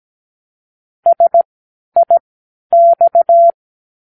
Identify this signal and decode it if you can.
Morse code, heard as a keyed tone, about 17 wpm, spaced as for 10 wpm, 680 Hz. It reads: SIX